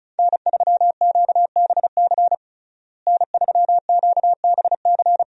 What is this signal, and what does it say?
Morse code, heard as a keyed tone, about 36 wpm, 700 Hz. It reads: N3QBC N3QBC
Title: USA Call Signs - 5 Characters - Repeated 2x - 35wpm